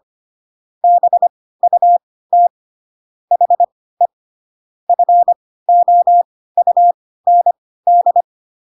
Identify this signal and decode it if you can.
Morse code, heard as a keyed tone, about 25 wpm, spaced as for 16 wpm, 700 Hz. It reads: BUT HE FOUND